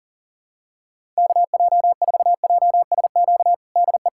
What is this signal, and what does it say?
Morse code, heard as a keyed tone, about 40 wpm, 700 Hz. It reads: KJ4JSQ DE
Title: USA Callsign DE Callsign 40wpm